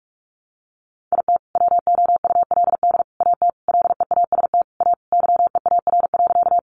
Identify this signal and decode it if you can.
Morse code, heard as a keyed tone, about 44 wpm, 700 Hz. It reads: IT WOULD AT LEAST A YEAR.
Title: Sentences from Top 700 Words 45wpm